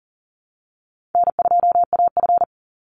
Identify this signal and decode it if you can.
Morse code, heard as a keyed tone, about 40 wpm, 700 Hz. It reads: N2AF
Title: NA SOTA Chaser Callsigns 40wpm